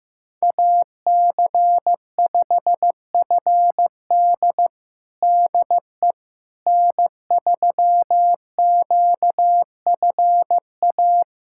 Morse code, heard as a keyed tone, about 15 wpm, 700 Hz. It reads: AC5FD DE N3QFA
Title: USA Callsign DE Callsign - 5 Characters - 15wpm